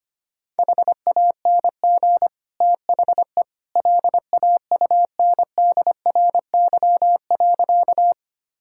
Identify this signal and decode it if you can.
Morse code, heard as a keyed tone, about 25 wpm, 700 Hz. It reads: HANG THE LAUNDRY.
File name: Easy Sentences with Hints - 25wpm